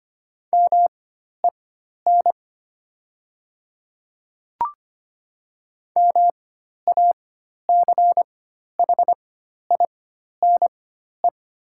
Morse code, heard as a keyed tone, about 25 wpm, spaced as for 12 wpm, 700 Hz. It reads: MEN MACHINE
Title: 4x Character Spacing - Top 500 Words - 25wpm